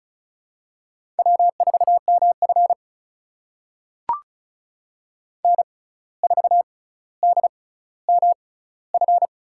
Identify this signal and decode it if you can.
Morse code, heard as a keyed tone, about 36 wpm, spaced as for 17 wpm, 700 Hz. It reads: W4MF N4DMF
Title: USA Call Signs - 6x Character Spacing - 35wpm